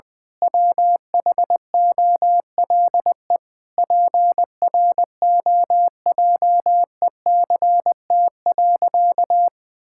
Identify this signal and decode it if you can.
Morse code, heard as a keyed tone, about 21 wpm, 700 Hz. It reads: WHOLE PROJECT.
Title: Sentences with 7-Word Phrases - 20wpm